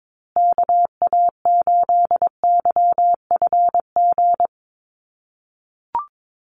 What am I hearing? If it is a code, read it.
Morse code, heard as a keyed tone, about 22 wpm, 700 Hz. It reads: KA8YFG
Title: USA Call Signs Repeated 2x 22wpm